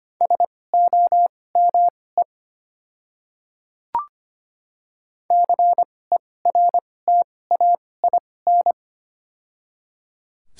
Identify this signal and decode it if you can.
Morse code, heard as a keyed tone, about 25 wpm, spaced as for 18 wpm, 690 Hz. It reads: SOME CERTAIN
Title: 2x Character Spacing - Top 500 Words - 25wpm